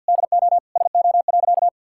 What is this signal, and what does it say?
Morse code, heard as a keyed tone, about 50 wpm, 700 Hz. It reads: DO SO.
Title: Sentences from Top 1000 Words - Review Part 5 - 50wpm